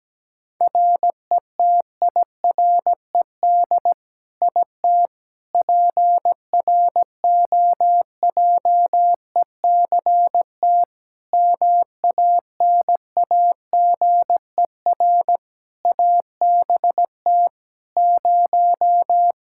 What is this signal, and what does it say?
Morse code, heard as a keyed tone, about 17 wpm, 700 Hz. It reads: RETIRED IT PROJECT MANAGER ABT 0
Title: Practice QSO - Exchange 3 - More Info - 17wpm